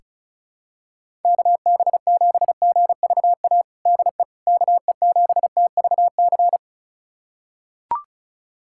Morse code, heard as a keyed tone, about 34 wpm, 700 Hz. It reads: KB7GVA DE KE7TVC
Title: USA Callsign DE Callsign - 6 Characters - 35wpm